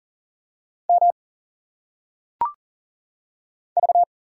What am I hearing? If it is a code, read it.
Morse code, heard as a keyed tone, about 40 wpm, 700 Hz. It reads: M V
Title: Single Letters - Phonetic Pronunciation 40wpm